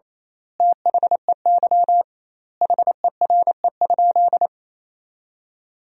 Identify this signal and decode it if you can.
Morse code, heard as a keyed tone, about 28 wpm, 700 Hz. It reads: THEY HERE?
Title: 2x Word-Spacing - Sentences from Top 200 Words - 28wpm